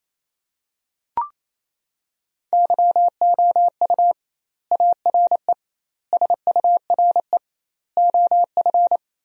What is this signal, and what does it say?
Morse code, heard as a keyed tone, about 28 wpm, 700 Hz. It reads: YOU ARE SURE OF
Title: 2x Word-Spacing - Sentences from Top 300 Words - 28wpm